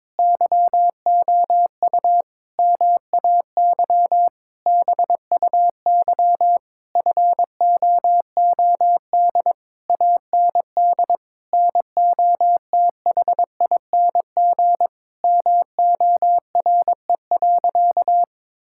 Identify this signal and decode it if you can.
Morse code, heard as a keyed tone, about 22 wpm, 700 Hz. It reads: YOU MAY BUY FOOD AND NOTHING MORE.
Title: Sentences from Top 1000 Words - Review Part 5 - 22wpm